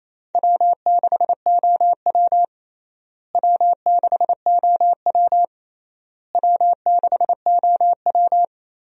Morse code, heard as a keyed tone, about 28 wpm, 700 Hz. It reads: W6OW W6OW W6OW
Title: USA Call Signs - 4 Characters - Repeated 3x - 28wpm